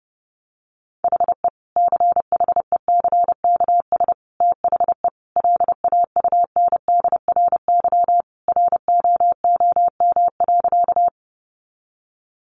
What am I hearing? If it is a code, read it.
Morse code, heard as a keyed tone, about 30 wpm, 700 Hz. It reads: HE CHECKS THE LAUNDRY ROOM.
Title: Easy Phrases 30wpm